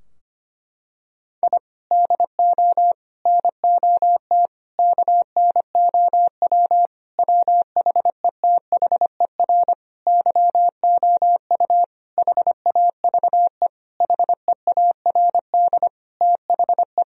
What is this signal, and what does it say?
Morse code, heard as a keyed tone, about 25 wpm, 700 Hz. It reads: I DO NOT KNOW WHETHER YOU HAVE HEARD THE
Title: Sentences from Top 1000 Words - Review Part 3 - 25wpm